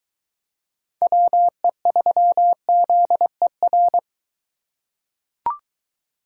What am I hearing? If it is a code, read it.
Morse code, heard as a keyed tone, about 23 wpm, 700 Hz. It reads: WE3ZER